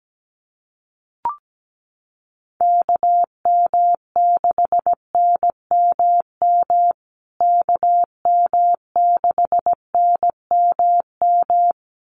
Morse code, heard as a keyed tone, about 17 wpm, 700 Hz. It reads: KM6NMM KM6NMM
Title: USA Call Signs Repeated 2x 17wpm